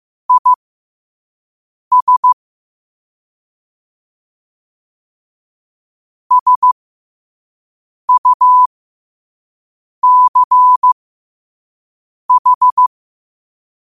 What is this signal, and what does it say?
Morse code, heard as a keyed tone, about 15 wpm, spaced as for 5 wpm, 1000 Hz. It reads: IS SUCH
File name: qotdfiles / 5wpm